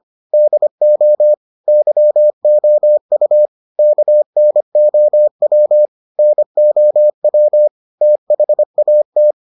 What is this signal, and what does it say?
Morse code, heard as a keyed tone, about 25 wpm, 600 Hz. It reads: DO YOU KNOW NOW THAT